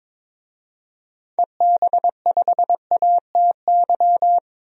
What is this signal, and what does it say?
Morse code, heard as a keyed tone, about 22 wpm, 700 Hz. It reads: EB5ATY
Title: DXCC-100 Call Signs - 6 Characters - 22wpm